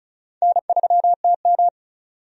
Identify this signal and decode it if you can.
Morse code, heard as a keyed tone, about 36 wpm, 700 Hz. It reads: N3TM
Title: USA Call Signs - 4 Characters - 35wpm